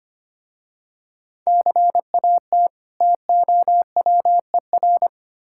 Morse code, heard as a keyed tone, about 25 wpm, 700 Hz. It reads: CAT TOWER
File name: practice / 25wpm / Difficult Words with Hints - 25wpm